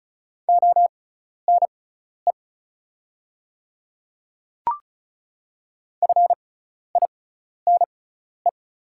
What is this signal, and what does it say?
Morse code, heard as a keyed tone, about 34 wpm, spaced as for 12 wpm, 690 Hz. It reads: ONE FINE
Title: 6x Character Spacing - Top 500 Words - 35wpm